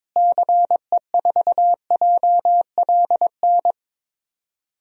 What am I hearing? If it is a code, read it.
Morse code, heard as a keyed tone, about 22 wpm, 700 Hz. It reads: CE4JLN
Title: DXCC-100 Call Signs - 6 Characters - Repeated 3x - 22wpm